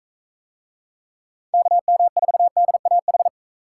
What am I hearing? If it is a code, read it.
Morse code, heard as a keyed tone, about 41 wpm, 700 Hz. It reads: KM4DAH